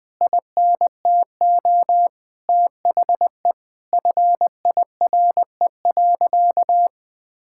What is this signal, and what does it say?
Morse code, heard as a keyed tone, about 20 wpm, 700 Hz. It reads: INTO THE FIRE.
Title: Sentences with 9-Word Phrases - 20wpm